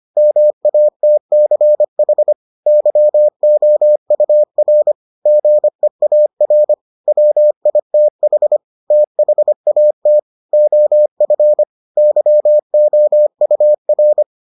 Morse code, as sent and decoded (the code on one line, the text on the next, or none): -- .- - -.-. .... / -.-- --- ..- .-. / --. .- .-. / .-- .. - .... / - .... .- - / --- ..-. / -.-- --- ..- .-.
MATCH YOUR GEAR WITH THAT OF YOUR